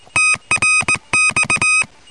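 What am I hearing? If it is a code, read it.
Morse code, heard as a keyed tone, about 19 wpm, 2550 Hz, with noise. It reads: TRX
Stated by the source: QRP contact between UA1CEG and UA3YLP using homemade gear on 14060 kHz during a QRP round table, with heavy interference from a contest.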